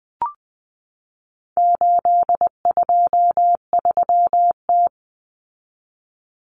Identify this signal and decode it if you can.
Morse code, heard as a keyed tone, about 20 wpm, 700 Hz. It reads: 823T
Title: Common Cut Numbers - 4-Digit 20wpm